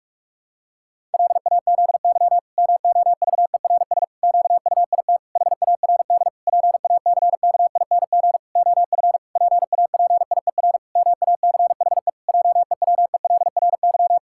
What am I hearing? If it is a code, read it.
Morse code, heard as a keyed tone, about 48 wpm, 700 Hz. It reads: LAZY MOVERS QUIT HARD PACKING OF PAPIER MACHE JEWELRY
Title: Pangram Challenge v2 - 45wpm